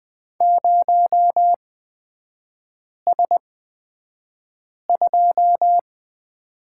Morse code, heard as a keyed tone, about 20 wpm, spaced as for 5 wpm, 700 Hz. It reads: 0S2
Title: 250 Lesson 40e - BK - Three Characters 20wpm